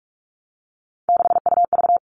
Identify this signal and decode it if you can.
Morse code, heard as a keyed tone, about 44 wpm, 700 Hz. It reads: 6UV
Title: All Cut Numbers - 3-Digit 45wpm